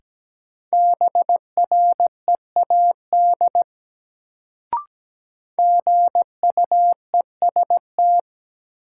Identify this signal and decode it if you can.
Morse code, heard as a keyed tone, about 17 wpm, 700 Hz. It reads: BREAD GUEST
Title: Sentences from Top 1600 Words - New Word Drill - 17wpm